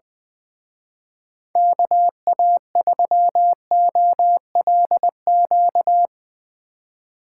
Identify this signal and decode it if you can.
Morse code, heard as a keyed tone, about 23 wpm, 700 Hz. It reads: KA3OLQ
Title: Speed Racing - USA 2x3 Call Signs 20wpm